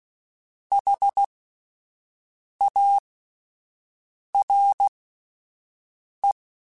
Morse code, heard as a keyed tone, about 16 wpm, spaced as for 5 wpm, 800 Hz. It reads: HARE